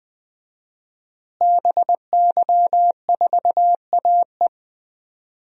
Morse code, heard as a keyed tone, about 20 wpm, 700 Hz. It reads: BY4AE
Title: DXCC-100 Call Signs - 5 Characters - Repeated 2x - 20wpm